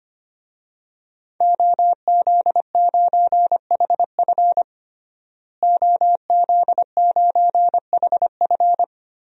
Morse code, heard as a keyed tone, about 25 wpm, 700 Hz. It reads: OZ9HF OZ9HF
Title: DXCC-100 Call Signs - 5 Characters - Repeated 3x - 25wpm